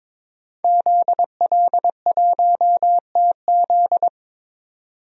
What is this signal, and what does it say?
Morse code, heard as a keyed tone, about 22 wpm, 700 Hz. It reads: ZL1TZ
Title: DXCC-100 Call Signs - 5 Characters - 22wpm